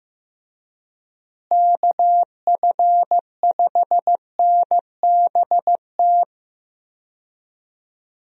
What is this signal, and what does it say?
Morse code, heard as a keyed tone, about 15 wpm, 700 Hz. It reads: KF5NBT
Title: USA Call Signs - 6 Characters - Repeated 3x - 15wpm